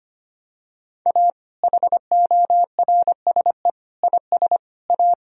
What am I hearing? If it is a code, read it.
Morse code, heard as a keyed tone, about 25 wpm, 700 Hz. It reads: A HORSE IS A
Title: Sentences from Top 1000 Words - Review Part 1 - 25wpm